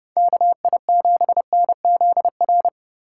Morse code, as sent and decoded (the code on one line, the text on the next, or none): -.- .. --... -. --.. .-.
KI7NZR